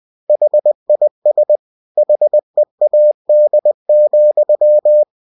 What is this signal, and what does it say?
Morse code, heard as a keyed tone, about 20 wpm, 600 Hz. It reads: HIS HEAD,